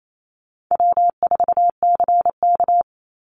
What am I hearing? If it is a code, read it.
Morse code, heard as a keyed tone, about 28 wpm, 700 Hz. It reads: W4CK
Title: Easy Call Signs Repeated 2x - 28wpm